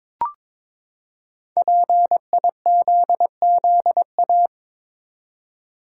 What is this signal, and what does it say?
Morse code, heard as a keyed tone, about 22 wpm, 700 Hz. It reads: PIZZA
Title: Five Letter Words - 22wpm